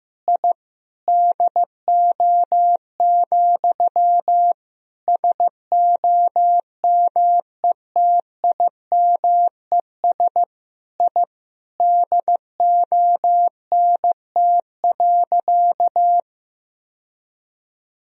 Morse code, heard as a keyed tone, about 16 wpm, 700 Hz. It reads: I DO, SOMETIMES I DONT.
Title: Sentences from Top 1000 Words - Review Part 3 - 15wpm